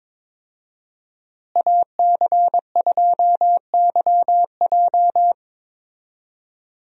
Morse code, heard as a keyed tone, about 22 wpm, 700 Hz. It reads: AC2YJ
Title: USA Call Signs - 5 Characters - Repeated 2x - 22wpm